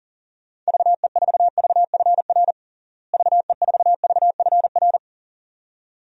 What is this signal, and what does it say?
Morse code, heard as a keyed tone, about 37 wpm, 700 Hz. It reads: VE4VFR VE4VFR
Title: DXCC-100 Call Signs - 6 Characters - Repeated 2x - 40wpm